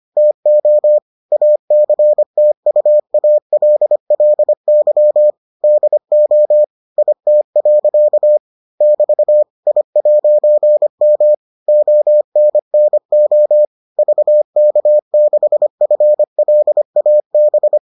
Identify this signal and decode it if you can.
Morse code, heard as a keyed tone, about 25 wpm, 600 Hz. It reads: TO ACTUALLY DO IT. <BT> I'M ONNO VK6FLAB